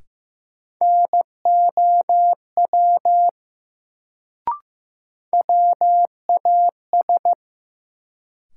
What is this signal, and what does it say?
Morse code, heard as a keyed tone, about 15 wpm, 700 Hz. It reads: NOW WAS
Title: Sentences from Top 100 Words - New Word Drill - 15wpm